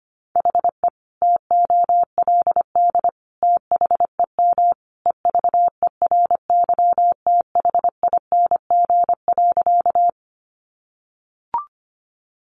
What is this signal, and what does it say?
Morse code, heard as a keyed tone, about 25 wpm, 700 Hz. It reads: HE TOLD THEM EVERYTHING.
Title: Sentences from Top 700 Words 25wpm